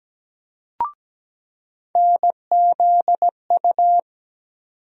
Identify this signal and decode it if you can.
Morse code, heard as a keyed tone, about 17 wpm, 700 Hz. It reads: NZU
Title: USA Call Sign Postfixes - 3 Letters 17wpm